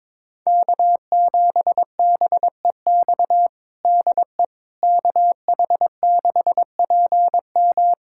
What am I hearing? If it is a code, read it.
Morse code, heard as a keyed tone, about 22 wpm, 700 Hz. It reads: K7BEX DE KH6PM
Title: USA Callsign DE Callsign - 5 Characters - 22wpm